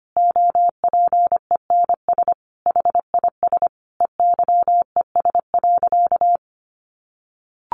Morse code, heard as a keyed tone, about 25 wpm, 700 Hz. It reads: OPENS HIS EYES.
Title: Easy Phrases 25wpm